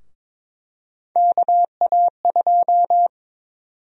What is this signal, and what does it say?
Morse code, heard as a keyed tone, about 21 wpm, 700 Hz. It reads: KA2
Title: USA Call Sign Prefixes - Repeated 2x - 22wpm